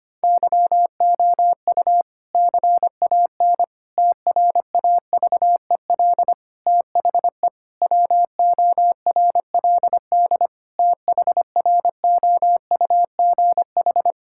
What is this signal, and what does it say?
Morse code, heard as a keyed tone, about 25 wpm, 700 Hz. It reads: YOU CAN TRAVEL THE WORLD THROUGH
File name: practice / 25wpm / Difficult Sentences with Hints - 25wpm